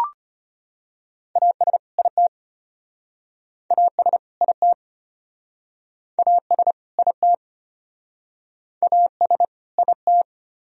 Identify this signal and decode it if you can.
Morse code, heard as a keyed tone, about 36 wpm, 700 Hz. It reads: AS IT AS IT AS IT AS IT